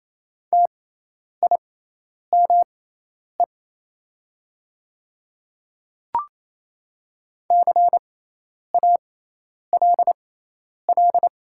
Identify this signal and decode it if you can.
Morse code, heard as a keyed tone, about 28 wpm, spaced as for 10 wpm, 690 Hz. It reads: TIME CALL
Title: 6x Character Spacing - Top 200 Words - 28wpm